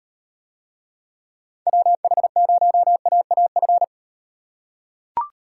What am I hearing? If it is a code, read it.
Morse code, heard as a keyed tone, about 34 wpm, 700 Hz. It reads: WH0AAF